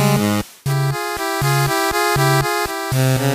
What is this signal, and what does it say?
There is no code.